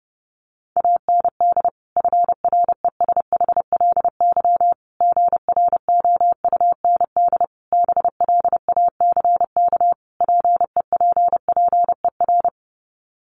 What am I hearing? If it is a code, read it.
Morse code, heard as a keyed tone, about 30 wpm, 700 Hz. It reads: AND FRESHLY GROUND BLACK PEPPER
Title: Sets of 5 Words Encore 30wpm